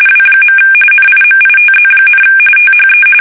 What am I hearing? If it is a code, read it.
RTTY, about 45.45 baud. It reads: W. NIMITZ, THE
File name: fsk850 / USS-Massachusetts